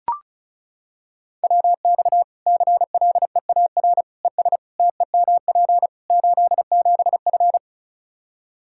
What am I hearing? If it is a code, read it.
Morse code, heard as a keyed tone, about 36 wpm, 700 Hz. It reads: WX CLEAR ES TEMP 87F
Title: Practice QSO - Exchange 2 - Info - 35wpm